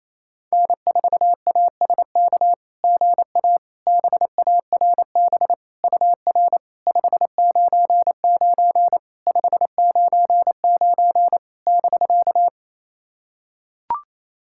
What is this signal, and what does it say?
Morse code, heard as a keyed tone, about 28 wpm, 700 Hz. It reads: N4ASK GA BARB UR 599 599 <BK>
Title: SOTA QSO Practice 28wpm